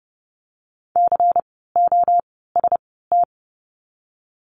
Morse code, heard as a keyed tone, about 30 wpm, spaced as for 17 wpm, 700 Hz. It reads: COST
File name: Lesson 17f - Letter u - Words 30wpm